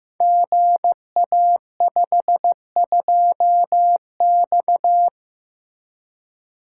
Morse code, heard as a keyed tone, about 15 wpm, 700 Hz. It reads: GA52X
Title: Code Groups - 5 Letter:Numbers - 15wpm